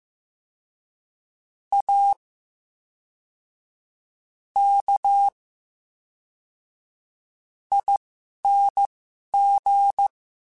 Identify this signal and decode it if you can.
Morse code, heard as a keyed tone, about 15 wpm, spaced as for 5 wpm, 800 Hz. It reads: AKING
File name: Special INGs / Special INGs-mp3s / VerySlowToFaster Ing @10